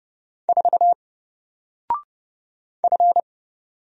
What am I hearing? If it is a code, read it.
Morse code, heard as a keyed tone, about 30 wpm, 700 Hz. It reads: 4 F
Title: Single Letter-Number Alt 30wpm